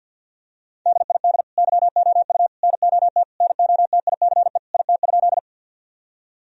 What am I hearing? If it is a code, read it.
Morse code, heard as a keyed tone, about 50 wpm, 700 Hz. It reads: DID YOU NOT NOTICE IT?
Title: Sentences from Top 1000 Words - Review Part 2 - 50wpm